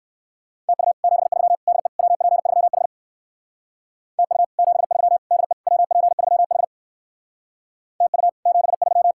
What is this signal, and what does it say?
Morse code, heard as a keyed tone, about 59 wpm, 700 Hz. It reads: TU 73 DE WP3H TU 73 DE WP3H TU 73